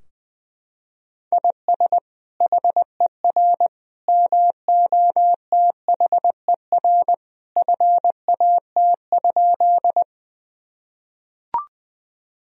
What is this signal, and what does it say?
Morse code, heard as a keyed tone, about 20 wpm, 700 Hz. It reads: IS HER MOTHER FAT?
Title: Sentences from Top 1000 Words - Review Part 4 - 20wpm